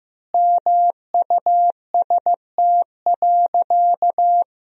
Morse code, heard as a keyed tone, about 16 wpm, 700 Hz. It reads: MUST.
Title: 4x Word-Spacing - Sentences from Top 200 Words - 15wpm